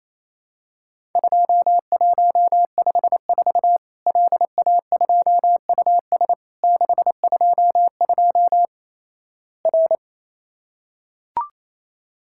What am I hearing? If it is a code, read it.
Morse code, heard as a keyed tone, about 28 wpm, 700 Hz. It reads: 2154 LA2US 622 R
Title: Worked All Europe DX Contest - QTC Exchange - 28wpm